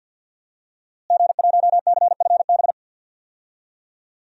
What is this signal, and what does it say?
Morse code, heard as a keyed tone, about 52 wpm, 700 Hz. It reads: G1CFB
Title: DXCC-100 Call Signs - 5 Characters - Repeated 2x - 50wpm